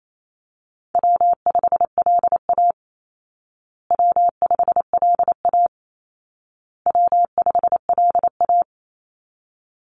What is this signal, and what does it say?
Morse code, heard as a keyed tone, about 28 wpm, 700 Hz. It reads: W5LA W5LA W5LA
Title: Call Signs Repeated 2x - 28wpm